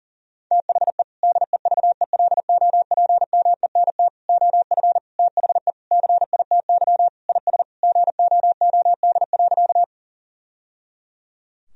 Morse code, heard as a keyed tone, about 40 wpm, 700 Hz. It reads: THE DEVELOPMENT OF THE CITY IS GOOD.
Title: Sentences from Top 1000 Words - Review Part 4 - 40wpm